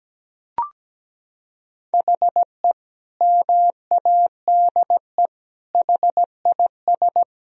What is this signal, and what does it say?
Morse code, heard as a keyed tone, about 17 wpm, 700 Hz. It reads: HE MADE HIS
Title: Sentences from Top 1000 Words - Review Part 4 - 17wpm